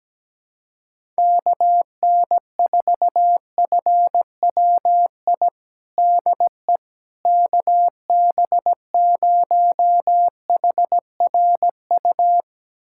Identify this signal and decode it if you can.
Morse code, heard as a keyed tone, about 17 wpm, 700 Hz. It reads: KN4FWI DE KB0HRU
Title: USA Callsign DE Callsign - 6 Characters - 17wpm